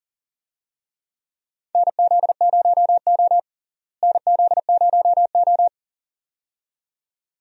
Morse code, heard as a keyed tone, about 40 wpm, 700 Hz. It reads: NZ0O NZ0O
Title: USA Call Signs - 4 Characters - Repeated 2x - 40wpm